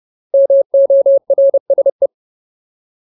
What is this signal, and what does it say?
Morse code, heard as a keyed tone, about 30 wpm, 550 Hz. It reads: MORSE
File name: Ultimate Word List / 30WPM Ultimate